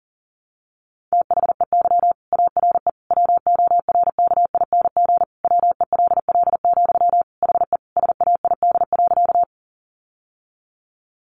Morse code, heard as a keyed tone, about 40 wpm, 700 Hz. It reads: THEY ARE WORKING WELL, HE SAID.
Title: Sentences from Top 700 Words 40wpm